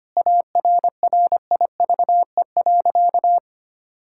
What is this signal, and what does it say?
Morse code, heard as a keyed tone, about 25 wpm, 700 Hz. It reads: ARRIVE.
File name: practice / 25wpm / Sentences with 3-Word Phrases - 25wpm